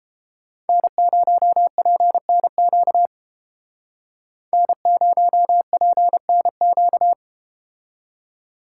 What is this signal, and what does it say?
Morse code, heard as a keyed tone, about 30 wpm, 700 Hz. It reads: N0PNQ N0PNQ